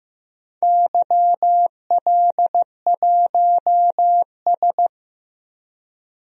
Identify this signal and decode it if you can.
Morse code, heard as a keyed tone, about 16 wpm, 700 Hz. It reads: YL1S
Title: DXCC-100 Call Signs - 4 Characters - 15wpm